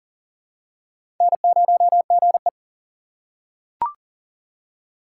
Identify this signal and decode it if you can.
Morse code, heard as a keyed tone, about 40 wpm, 700 Hz. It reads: N0GE
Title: USA Call Signs - 4 Characters - 3x Character Spacing - 40wpm